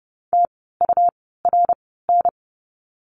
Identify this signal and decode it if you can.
Morse code, heard as a keyed tone, about 30 wpm, spaced as for 17 wpm, 700 Hz. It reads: TURN
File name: Lesson 17f - Letter u - Words 30wpm